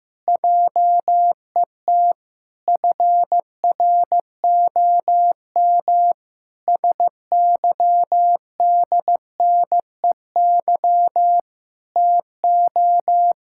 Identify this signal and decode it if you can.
Morse code, heard as a keyed tone, about 15 wpm, 700 Hz. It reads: JET FROM SYDNEY TO
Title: Pangram Challenge v2 - 15wpm